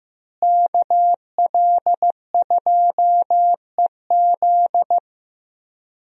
Morse code, heard as a keyed tone, about 15 wpm, 700 Hz. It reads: KL2EZ
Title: USA Call Signs - 5 Characters - Repeated 3x - 15wpm